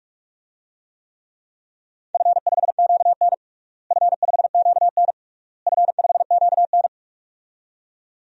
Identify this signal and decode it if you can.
Morse code, heard as a keyed tone, about 47 wpm, 700 Hz. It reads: F5QN F5QN F5QN
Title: DXCC-100 Call Signs - 4 Characters - Repeated 3x - 45wpm